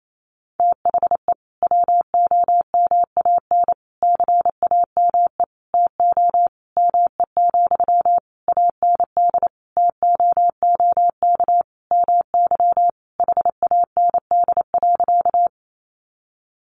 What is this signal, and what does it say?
Morse code, heard as a keyed tone, about 28 wpm, 700 Hz. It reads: THE WOMAN CAME TO ME, AND TOOK MY HAND.
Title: Sentences from Top 1100 Words 28wpm